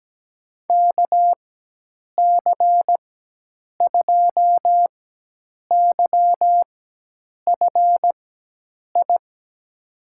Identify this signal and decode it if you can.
Morse code, heard as a keyed tone, about 17 wpm, spaced as for 8 wpm, 700 Hz. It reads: KC2YFI
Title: USA Call Signs - 6 Characters - 4x Character Spacing - 17wpm